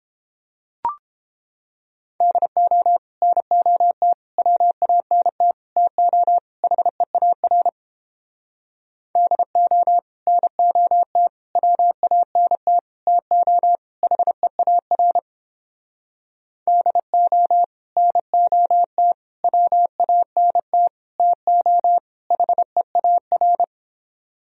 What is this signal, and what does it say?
Morse code, heard as a keyed tone, about 29 wpm, 700 Hz. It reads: DO NOT WANT TO HEAR DO NOT WANT TO HEAR DO NOT WANT TO HEAR